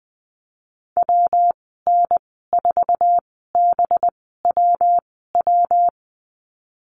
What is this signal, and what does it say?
Morse code, heard as a keyed tone, about 20 wpm, spaced as for 14 wpm, 700 Hz. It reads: WN4BWW
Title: 213 Lesson 34g - Number 8 - Call Signs 20wpm